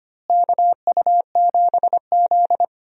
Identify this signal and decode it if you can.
Morse code, heard as a keyed tone, about 25 wpm, 700 Hz. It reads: KU7Z
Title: USA Call Signs - 4 Characters - Repeated 3x - 25wpm